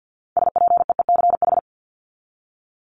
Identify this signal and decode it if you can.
Morse code, heard as a keyed tone, about 50 wpm, 700 Hz. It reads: SPEECH
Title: Top 1751-2000 Words 50wpm